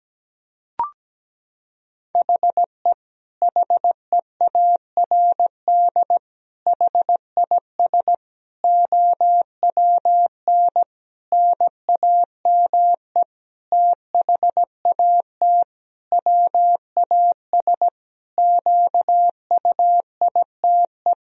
Morse code, heard as a keyed tone, about 17 wpm, 700 Hz. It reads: HE HEARD HIS OWN NAME THAT WAS QUITE
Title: Sentences from Top 1000 Words - Review Part 2 - 17wpm